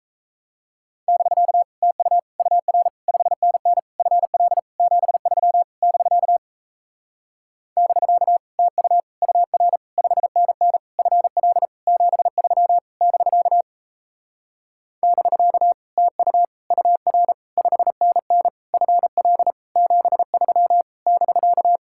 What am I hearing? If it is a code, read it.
Morse code, heard as a keyed tone, about 37 wpm, 700 Hz. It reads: <BK> TU UR 5NN FL 73 <BK> <BK> TU UR 5NN FL 73 <BK> <BK> TU UR 5NN FL 73 <BK>